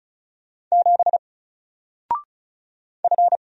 Morse code, heard as a keyed tone, about 35 wpm, 700 Hz. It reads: 7 F
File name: Single-Letter-Number-35wpm